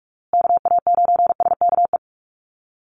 Morse code, heard as a keyed tone, about 44 wpm, 700 Hz. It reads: KA9SKE
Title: USA Call Signs Repeated 2x 45wpm